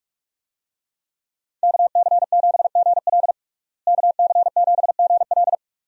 Morse code, heard as a keyed tone, about 47 wpm, 700 Hz. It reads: KC7GL KC7GL
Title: USA Call Signs - 5 Characters - Repeated 2x - 45wpm